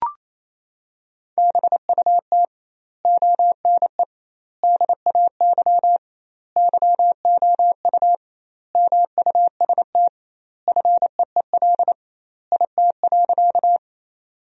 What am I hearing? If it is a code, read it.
Morse code, heard as a keyed tone, about 28 wpm, 700 Hz. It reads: BUT ONE DAY YOU MUST FEEL IT.
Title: 2x Word-Spacing - Sentences from Top 400 Words - 28wpm